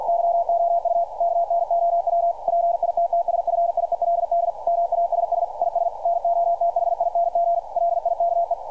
Morse code, heard as a keyed tone, about 50 wpm, 690 Hz, with radio noise. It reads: YOU KNOW PITTSFIELD MASS IS NOTHING LIKE
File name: archived / CRSnet